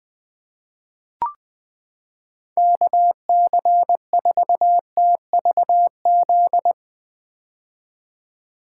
Morse code, heard as a keyed tone, about 20 wpm, 700 Hz. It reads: KC4TVZ
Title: 2023 POTA Activator Call Signs - 20wpm